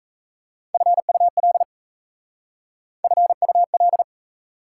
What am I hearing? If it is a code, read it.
Morse code, heard as a keyed tone, about 35 wpm, 690 Hz. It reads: FUL FUL
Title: Speed-Racing Prefixes and Suffixes 28wpm